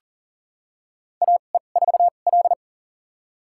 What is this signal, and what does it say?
Morse code, heard as a keyed tone, about 40 wpm, spaced as for 29 wpm, 700 Hz. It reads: AE4L
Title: USA Call Signs - 2x Character Spacing - 40wpm